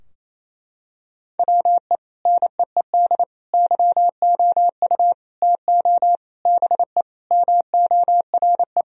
Morse code, heard as a keyed tone, about 28 wpm, 700 Hz. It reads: WE NEED YOU TO BE MORE
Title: Sentences from Top 1000 Words - Review Part 3 - 28wpm